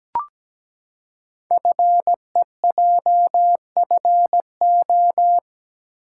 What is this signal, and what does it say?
Morse code, heard as a keyed tone, about 17 wpm, 700 Hz. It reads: FEJFO
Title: Code Groups - 5 Letters - 17wpm